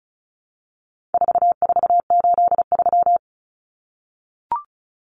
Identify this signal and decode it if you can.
Morse code, heard as a keyed tone, about 34 wpm, 700 Hz. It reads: 4483
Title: Common Cut Numbers - 4-Digit 35wpm